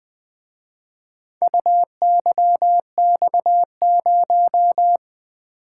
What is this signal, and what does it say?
Morse code, heard as a keyed tone, about 23 wpm, 700 Hz. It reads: UYX0
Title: Speed-Racing - Four Letter-Number Combinations - 20wpm